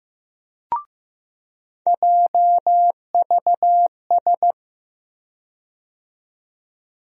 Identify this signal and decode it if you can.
Morse code, heard as a keyed tone, about 15 wpm, 700 Hz. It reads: JVS
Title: USA Call Sign Postfixes - 3 Letters 15wpm